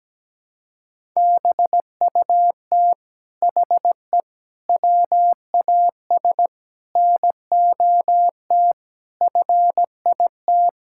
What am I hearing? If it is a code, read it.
Morse code, heard as a keyed tone, about 17 wpm, 700 Hz. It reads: BUT HE WAS NOT FIT
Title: Sentences from Top 1000 Words - Review Part 6 - 17wpm